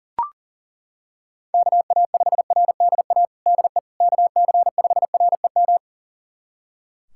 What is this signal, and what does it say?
Morse code, heard as a keyed tone, about 40 wpm, 700 Hz. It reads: KA5RDA DE KC5REM
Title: USA Callsign DE Callsign 40wpm